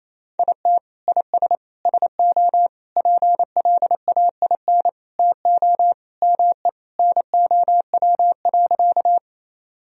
Morse code, heard as a keyed tone, about 28 wpm, 700 Hz. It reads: IT IS SO PLAIN TO ME NOW.
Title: Sentences from Top 1000 Words - Review Part 2 - 28wpm